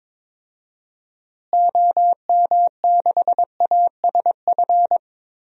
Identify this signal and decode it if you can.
Morse code, heard as a keyed tone, about 21 wpm, 700 Hz. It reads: OM6ASF
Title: DXCC-100 Call Signs - 6 Characters - Repeated 3x - 22wpm